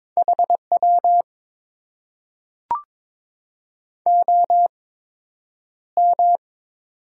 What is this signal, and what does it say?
Morse code, heard as a keyed tone, about 22 wpm, spaced as for 9 wpm, 700 Hz. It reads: HW OM
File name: Top 90 QSO Elements - 8x Character Spacing - 22wpm